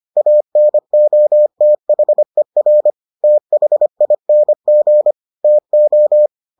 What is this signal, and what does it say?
Morse code, heard as a keyed tone, about 25 wpm, 600 Hz. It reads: ANOTHER THING TO